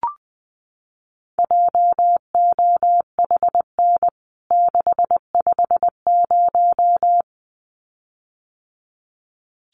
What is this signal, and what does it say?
Morse code, heard as a keyed tone, about 20 wpm, 700 Hz. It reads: JOHN 650
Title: CWT Exchange 20wpm